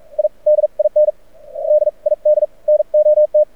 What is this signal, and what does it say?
Morse code, heard as a keyed tone, about 44 wpm, spaced as for 37 wpm, 600 Hz. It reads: I DIN DID NOT
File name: archived / CRSnet